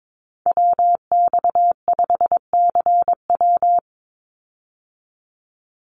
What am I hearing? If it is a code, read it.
Morse code, heard as a keyed tone, about 22 wpm, 700 Hz. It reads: WX5CW
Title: Call Signs Repeated 3x - 22wpm